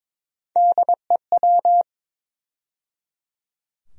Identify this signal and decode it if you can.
Morse code, heard as a keyed tone, about 22 wpm, 700 Hz. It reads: DEW
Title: USA Call Sign Postfixes - 3 Letters 22wpm